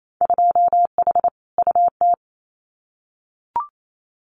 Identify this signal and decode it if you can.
Morse code, heard as a keyed tone, about 28 wpm, 700 Hz. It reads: 2H UT
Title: Winter Field Day Exchanges Encore 28wpm